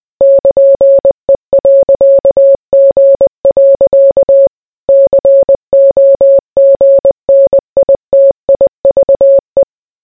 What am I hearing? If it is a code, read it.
Morse code, heard as a keyed tone, about 20 wpm, 550 Hz. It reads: <KN>E.G. COGNITIVE